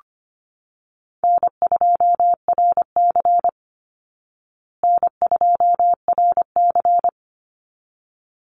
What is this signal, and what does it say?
Morse code, heard as a keyed tone, about 25 wpm, 700 Hz. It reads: N2RC N2RC
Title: Easy Call Signs Repeated 2x - 25wpm